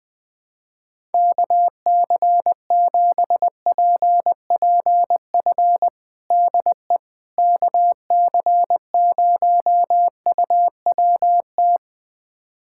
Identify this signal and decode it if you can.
Morse code, heard as a keyed tone, about 21 wpm, 700 Hz. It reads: KC7PPF DE KC0UWT
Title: USA Callsign DE Callsign - 6 Characters - 20wpm